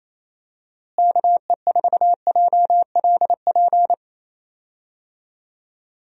Morse code, heard as a keyed tone, about 27 wpm, 700 Hz. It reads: KE4JLP
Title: USA Call Signs - 6 Characters - Repeated 3x - 28wpm